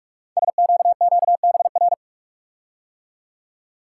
Morse code, heard as a keyed tone, about 44 wpm, 700 Hz. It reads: SQQBR
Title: Code Groups - 5 Letters - 45wpm